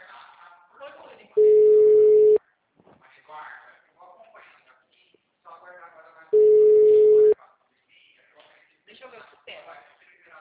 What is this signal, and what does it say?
Ringback tone (Europe, 425 Hz), with speech.
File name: out- -2011- - - .42938